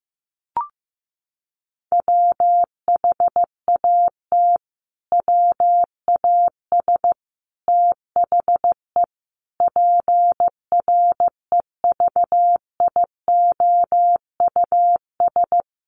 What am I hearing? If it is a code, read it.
Morse code, heard as a keyed tone, about 14 wpm, 700 Hz. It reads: WHAT WAS THE PREVIOUS
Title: Sentences from Top 1500 Words 15wpm